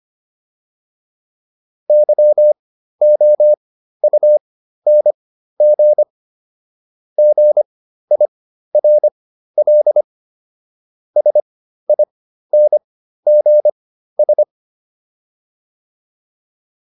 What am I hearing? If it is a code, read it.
Morse code, heard as a keyed tone, about 25 wpm, spaced as for 13 wpm, 600 Hz. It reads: YOUNG GIRL SINGS
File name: PR402 13